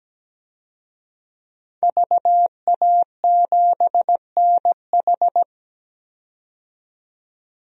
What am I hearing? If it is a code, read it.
Morse code, heard as a keyed tone, about 17 wpm, 700 Hz. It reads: VA7NH
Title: DXCC-100 Call Signs - 5 Characters - 17wpm